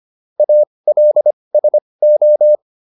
Morse code, heard as a keyed tone, about 25 wpm, spaced as for 20 wpm, 600 Hz. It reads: ALSO